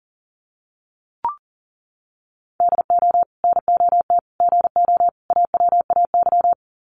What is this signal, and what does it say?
Morse code, heard as a keyed tone, about 40 wpm, 700 Hz. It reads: DO NOT GO AWAY
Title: Sets of 4 Words 40wpm